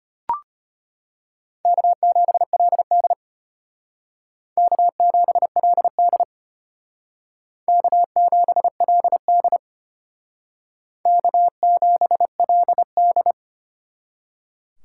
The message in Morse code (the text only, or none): K7LD K7LD K7LD K7LD